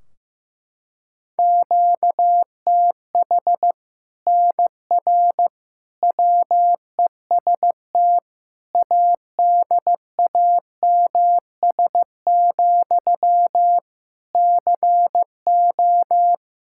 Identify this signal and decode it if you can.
Morse code, heard as a keyed tone, about 16 wpm, 700 Hz. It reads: QTH NR WEST ADAMS, CO
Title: Practice QSO - Exchange 1 - Beginning - 15wpm